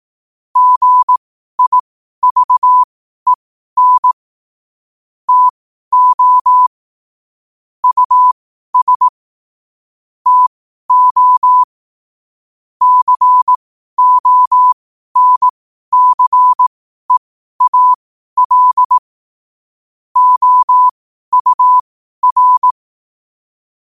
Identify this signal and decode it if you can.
Morse code, heard as a keyed tone, about 18 wpm, spaced as for 12 wpm, 1000 Hz. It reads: GIVEN TO US TO CONCEAL OUR
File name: qotdfiles / 12wpm